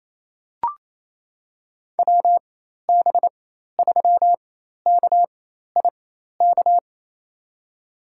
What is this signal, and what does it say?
Morse code, heard as a keyed tone, about 27 wpm, spaced as for 13 wpm, 700 Hz. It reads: WB3KIK
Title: USA Call Signs - 6 Characters - 4x Character Spacing - 28wpm